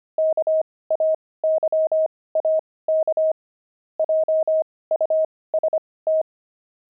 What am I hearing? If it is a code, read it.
Morse code, heard as a keyed tone, about 25 wpm, spaced as for 18 wpm, 630 Hz. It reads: KAYAK JUST
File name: pangram-fs-18-cs-25-explorer